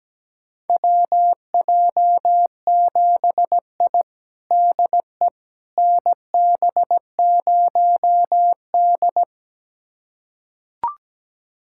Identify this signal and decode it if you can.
Morse code, heard as a keyed tone, about 17 wpm, 700 Hz. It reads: WJ7I DE NB0D
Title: USA Callsign DE Callsign - 4 Characters - 17wpm